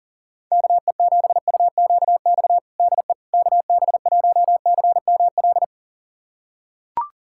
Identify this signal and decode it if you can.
Morse code, heard as a keyed tone, about 40 wpm, 700 Hz. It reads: KE7UQX DE KB1CML
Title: USA Callsign DE Callsign 40wpm